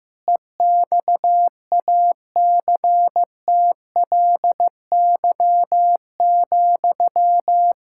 Morse code, heard as a keyed tone, about 15 wpm, 700 Hz. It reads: EXACTLY,
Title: Sentences from Top 1000 Words - Review Part 6 - 15wpm